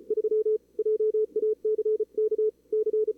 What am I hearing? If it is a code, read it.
Morse code, heard as a keyed tone, about 35 wpm, 440 Hz. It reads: 3 JACK C
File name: CWSprint / Feb2022 / k5tr-feb-na-sprint-cw-k5ot -2300